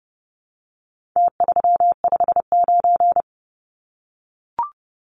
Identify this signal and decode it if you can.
Morse code, heard as a keyed tone, about 30 wpm, 700 Hz. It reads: T359
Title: Numbers - Contest Style - 4-Digit 30wpm